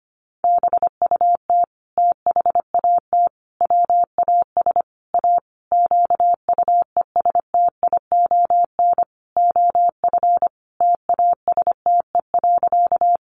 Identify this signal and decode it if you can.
Morse code, heard as a keyed tone, about 25 wpm, 700 Hz. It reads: BUT THAT WAS A QUESTION OF TASTE.
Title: Sentences from Top 1100 Words 25wpm